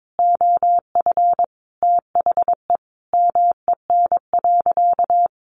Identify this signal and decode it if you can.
Morse code, heard as a keyed tone, about 22 wpm, 700 Hz. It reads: OF THE MEN.
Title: Sentences from Top 1400 Words 22wpm